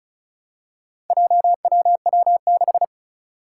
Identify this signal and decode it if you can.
Morse code, heard as a keyed tone, about 36 wpm, 700 Hz. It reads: JWW6
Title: Four Challenging Letter-Number Combinations 35wpm